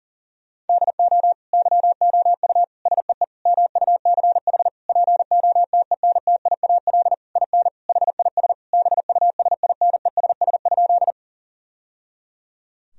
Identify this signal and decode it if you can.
Morse code, heard as a keyed tone, about 40 wpm, 700 Hz. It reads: DO YOU SEE MUCH POTENTIAL IN HIS BUSINESS?
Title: Sentences from Top 1000 Words - Review Part 4 - 40wpm